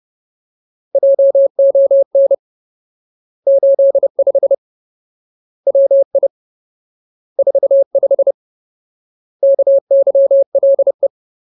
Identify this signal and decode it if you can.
Morse code, heard as a keyed tone, about 30 wpm, 550 Hz. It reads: JON 85 WI 45 KYLE
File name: Ultimate Word List / 30WPM Ultimate